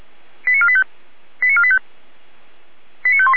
Five-tone selective call (ZVEI), heard as quick runs of five tones, tones of about 70 ms. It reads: 89485 89485 89518